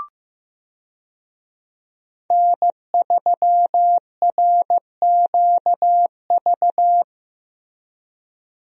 Morse code, heard as a keyed tone, about 15 wpm, 700 Hz. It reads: N3RQV
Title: USA Call Signs - 5 Characters - Repeated 3x - 15wpm